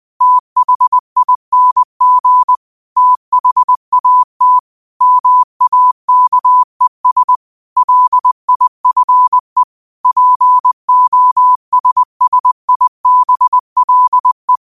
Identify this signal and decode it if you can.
Morse code, heard as a keyed tone, about 20 wpm, 1000 Hz. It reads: THING THAT MAKES LIFE POSSIBLE